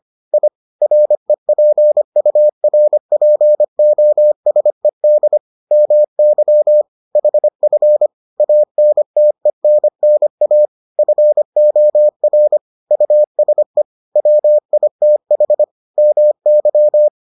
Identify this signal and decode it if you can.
Morse code, heard as a keyed tone, about 25 wpm, 600 Hz. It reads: I REPURPOSED MY HF ANTENNA FOR USE WITH MY